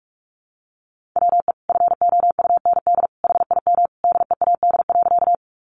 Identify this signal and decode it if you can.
Morse code, heard as a keyed tone, about 44 wpm, 700 Hz. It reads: WE FOUND HIM DEAD.
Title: Sentences from Top 1500 Words 45wpm